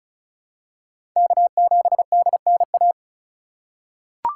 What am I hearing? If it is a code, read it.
Morse code, heard as a keyed tone, about 36 wpm, 700 Hz. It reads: K7DNA
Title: USA Call Signs - 5 Characters - 2x Character Spacing - 35wpm